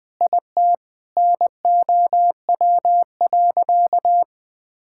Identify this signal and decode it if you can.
Morse code, heard as a keyed tone, about 20 wpm, 700 Hz. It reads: IT NOW.
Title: Sentences with 7-Word Phrases - Spoken Beginning - 20wpm